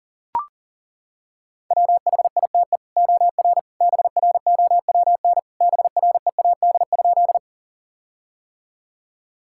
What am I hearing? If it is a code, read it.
Morse code, heard as a keyed tone, about 37 wpm, 700 Hz. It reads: WHITE OR BROWN BREAD?
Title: Sentences from Top 1600 Words 40wpm